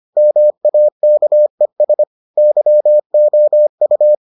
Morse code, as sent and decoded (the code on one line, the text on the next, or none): -- .- -.- . ... / -.-- --- ..-
MAKES YOU